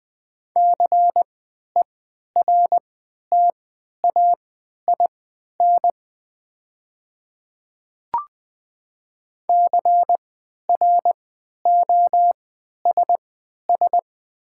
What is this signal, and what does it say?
Morse code, heard as a keyed tone, about 20 wpm, spaced as for 11 wpm, 700 Hz. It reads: CERTAIN CROSS
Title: 3x Character Spacing - Top 500 Words - 20wpm